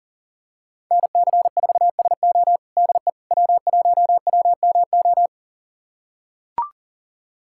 Morse code, heard as a keyed tone, about 40 wpm, 700 Hz. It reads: NC4SO DE W1WMO
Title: USA Callsign DE Callsign - 5 Characters - 40wpm